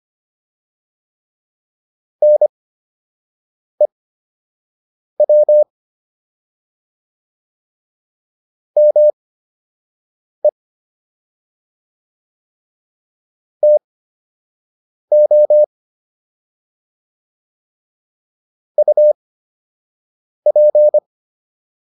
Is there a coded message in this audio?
Yes: Morse code from 2.2 to 21.0 s, heard as a keyed tone, about 25 wpm, spaced as for 6 wpm, 600 Hz. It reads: NEW ME TO UP